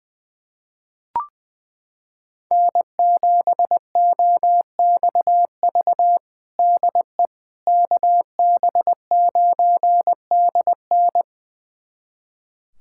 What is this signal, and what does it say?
Morse code, heard as a keyed tone, about 21 wpm, 700 Hz. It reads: N7OXV DE KB9DN
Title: USA Callsign DE Callsign - 5 Characters - 20wpm